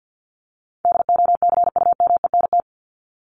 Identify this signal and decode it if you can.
Morse code, heard as a keyed tone, about 50 wpm, 700 Hz. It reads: DOCUMENT
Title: Top 1751-2000 Words 50wpm